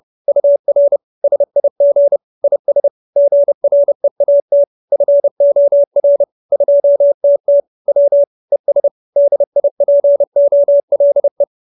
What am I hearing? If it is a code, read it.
Morse code, heard as a keyed tone, about 30 wpm, 570 Hz. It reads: UR SIG IS GREAT FOR 2TT W ES DIPOLE
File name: QSO317 30